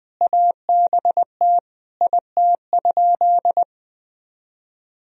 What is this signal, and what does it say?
Morse code, heard as a keyed tone, about 20 wpm, 700 Hz. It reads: ABT IT?
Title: Sentences from Top 200 Words and Abbreviations 20wpm